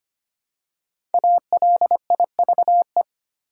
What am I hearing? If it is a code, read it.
Morse code, heard as a keyed tone, about 25 wpm, 700 Hz. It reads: ALIVE
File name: 5 Letter Words - First 3 Letters Given - 25wpm